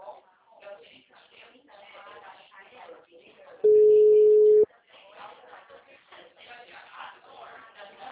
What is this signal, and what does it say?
Ringback tone (Europe, 425 Hz), with speech.
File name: out- -2002- - - .42857